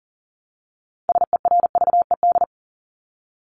Morse code, heard as a keyed tone, about 40 wpm, 700 Hz. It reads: SERVED
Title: ED Suffix - Top 100 Words with Base Words 40wpm